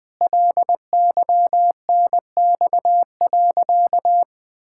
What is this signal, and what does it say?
Morse code, heard as a keyed tone, about 21 wpm, 700 Hz. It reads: LYNX.
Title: Pangram Challenge v2 - 20wpm